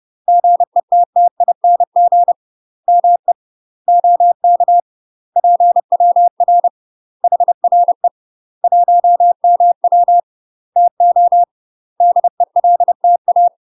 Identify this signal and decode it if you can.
Morse code, heard as a keyed tone, about 30 wpm, 700 Hz. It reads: GETTING ME OK PWR HRE 1MW TO DELTA
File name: UK MASTER QSOs TRAINING / 30WPM / UK MASTER QSOs 0000